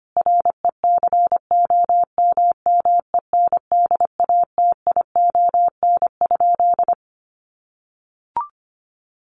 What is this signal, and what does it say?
Morse code, heard as a keyed tone, about 25 wpm, 700 Hz. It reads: RECOMMENDATION?
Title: Sentences from Top 1400 Words 25wpm